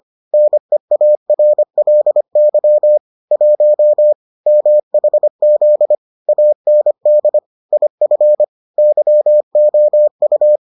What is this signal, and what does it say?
Morse code, heard as a keyed tone, about 25 wpm, 600 Hz. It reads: NEARLY 1 MHZ AND IF YOU